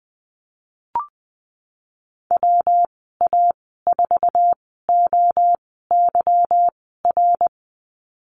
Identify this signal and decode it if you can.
Morse code, heard as a keyed tone, about 20 wpm, spaced as for 15 wpm, 700 Hz. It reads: WA4OYR
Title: 147 Lesson 24g - Letter y - Call Signs 20wpm